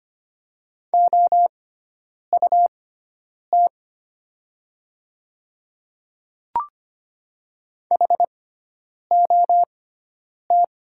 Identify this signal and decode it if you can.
Morse code, heard as a keyed tone, about 25 wpm, spaced as for 9 wpm, 700 Hz. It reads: OUT HOT
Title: 6x Character Spacing - Top 100 Words - 25wpm